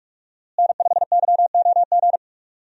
Morse code, heard as a keyed tone, about 46 wpm, 700 Hz. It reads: N5YOG
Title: USA Call Signs - 5 Characters - Repeated 2x - 45wpm